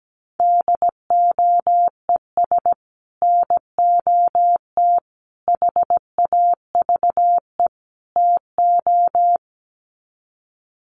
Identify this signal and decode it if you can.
Morse code, heard as a keyed tone, about 17 wpm, 700 Hz. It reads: DOES NOT HAVE TO
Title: Sets of 4 Words 17wpm